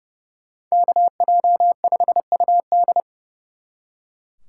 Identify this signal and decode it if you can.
Morse code, heard as a keyed tone, about 35 wpm, 700 Hz. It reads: KJ5UD
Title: Speed Racing - USA 2x2 Call Signs 30wpm